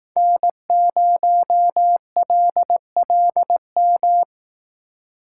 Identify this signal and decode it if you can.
Morse code, heard as a keyed tone, about 20 wpm, 700 Hz. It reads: N0LLM